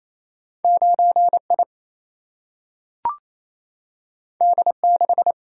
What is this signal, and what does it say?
Morse code, heard as a keyed tone, about 28 wpm, 700 Hz. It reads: 9I D6
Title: Single-Letter Single-Number 28wpm